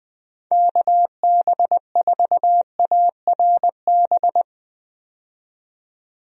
Morse code, heard as a keyed tone, about 21 wpm, 700 Hz. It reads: KB4ARB
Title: USA Call Signs - 6 Characters - 20wpm